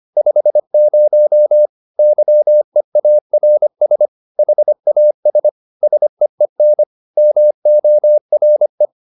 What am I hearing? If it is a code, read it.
Morse code, heard as a keyed tone, about 25 wpm, 600 Hz. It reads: 50 YEARS HAS SEEN MORE